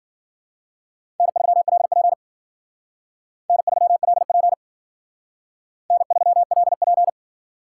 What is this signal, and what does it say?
Morse code, heard as a keyed tone, about 51 wpm, 700 Hz. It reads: N3LP N3LP N3LP